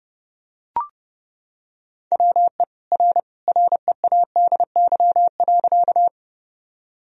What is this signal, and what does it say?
Morse code, heard as a keyed tone, about 30 wpm, 700 Hz. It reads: WE R READY.
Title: Sentences from Top 400 Words and Abbreviations 30wpm